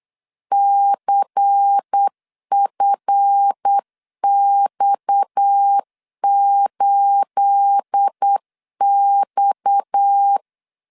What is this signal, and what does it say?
Morse code, heard as a keyed tone, about 8 wpm, 800 Hz. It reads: CFX8X
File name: CW / DET / 1 9 1500C S A 8wpm 5637